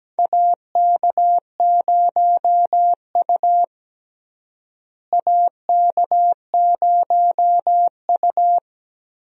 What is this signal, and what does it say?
Morse code, heard as a keyed tone, about 17 wpm, 700 Hz. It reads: AK0U AK0U
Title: USA Call Signs - 4 Characters - Repeated 3x - 17wpm